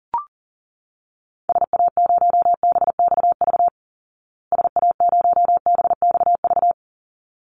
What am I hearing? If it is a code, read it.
Morse code, heard as a keyed tone, about 40 wpm, 700 Hz. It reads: SA0BXV SA0BXV
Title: Call Signs Repeated 3x - 40wpm